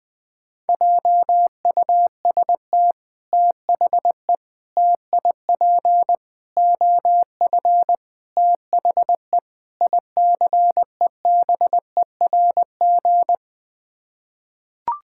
Morse code, heard as a keyed tone, about 21 wpm, 700 Hz. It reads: JUST THE TIP OF THE ICEBERG
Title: Sets of 6 Words - 20wpm